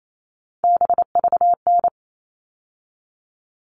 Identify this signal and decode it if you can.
Morse code, heard as a keyed tone, about 28 wpm, 700 Hz. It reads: BVN
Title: All Cut Numbers - 3-Digit 28wpm